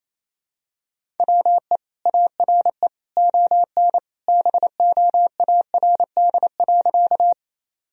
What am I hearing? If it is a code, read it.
Morse code, heard as a keyed tone, about 28 wpm, 700 Hz. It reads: WE ARE ON BOARD.
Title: Sentences from Top 1000 Words - Review Part 4 - 28wpm